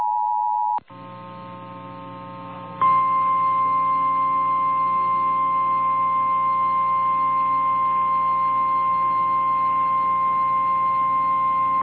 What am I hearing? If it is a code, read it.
Emergency Alert System (SAME) tones (the 853 and 960 Hz attention tone). It is heard from 0.3 to 11.6 s.
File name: eas / KNCYKBIE / KNCYKBIE